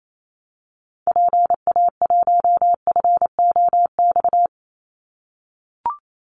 Morse code, heard as a keyed tone, about 28 wpm, 700 Hz. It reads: PA1FOX
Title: Call Signs - 28wpm